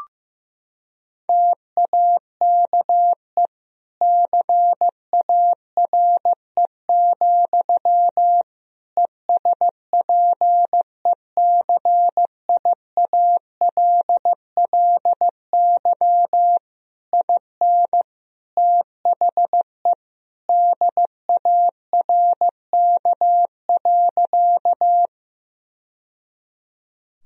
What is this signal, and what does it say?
Morse code, heard as a keyed tone, about 15 wpm, 700 Hz. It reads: TAKE CARE, ESPECIALLY IN THE DARK.
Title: Sentences from Top 1000 Words - Review Part 4 - 15wpm